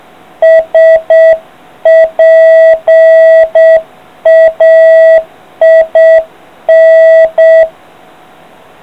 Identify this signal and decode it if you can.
Morse code, heard as a keyed tone, about 7 wpm, 650 Hz. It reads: SPAIN